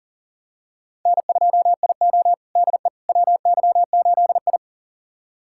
Morse code, heard as a keyed tone, about 37 wpm, 700 Hz. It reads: N2IO DE WY8I
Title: USA Callsign DE Callsign - 4 Characters - 40wpm